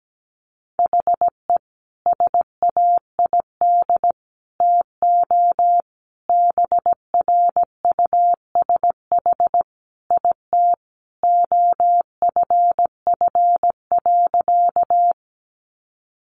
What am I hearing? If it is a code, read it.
Morse code, heard as a keyed tone, about 17 wpm, 700 Hz. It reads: HE SAID TO BRUSH IT OFF.
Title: Sentences from Top 1100 Words 17wpm